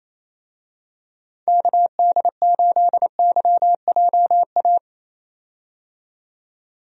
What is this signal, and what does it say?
Morse code, heard as a keyed tone, about 27 wpm, 700 Hz. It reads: KD8YJA
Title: USA Call Signs - 6 Characters - 28wpm